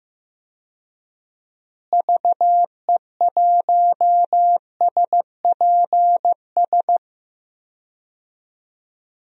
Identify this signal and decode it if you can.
Morse code, heard as a keyed tone, about 15 wpm, 700 Hz. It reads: VE1SPS
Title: DXCC-100 Call Signs - 6 Characters - Repeated 2x - 15wpm